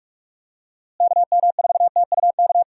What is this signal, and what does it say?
Morse code, heard as a keyed tone, about 46 wpm, 700 Hz. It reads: KM4TUK
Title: USA Call Signs - 6 Characters - Repeated 3x - 45wpm